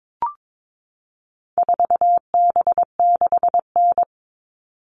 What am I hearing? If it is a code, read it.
Morse code, heard as a keyed tone, about 22 wpm, 700 Hz. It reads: 4B6N
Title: All Cut Numbers - 4-Digit 22wpm